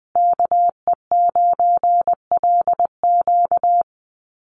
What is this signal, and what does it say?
Morse code, heard as a keyed tone, about 20 wpm, 700 Hz. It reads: KE9LQ
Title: USA Call Signs Repeated 3x 20wpm